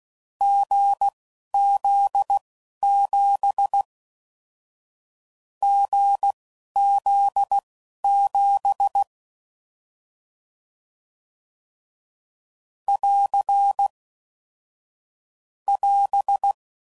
Morse code, heard as a keyed tone, about 16 wpm, spaced as for 11 wpm, 800 Hz. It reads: GZ7 GZ7 <AR> <AS>
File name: Mixed / Short Runs / awjndb6gz7